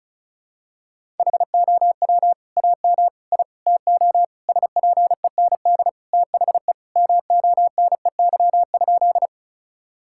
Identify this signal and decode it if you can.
Morse code, heard as a keyed tone, about 34 wpm, 700 Hz. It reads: HOW AM I TO SPEND THE MONEY?
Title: Sentences from Top 1000 Words - Review Part 5 - 35wpm